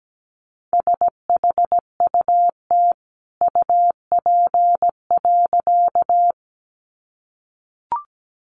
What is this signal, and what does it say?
Morse code, heard as a keyed tone, about 17 wpm, 700 Hz. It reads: SHUT UP.
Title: A Sound Of Thunder 17wpm